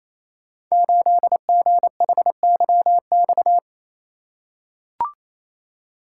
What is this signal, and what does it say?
Morse code, heard as a keyed tone, about 29 wpm, 700 Hz. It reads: 8GHYX
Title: Code Groups - 5 Letter:Numbers - 28wpm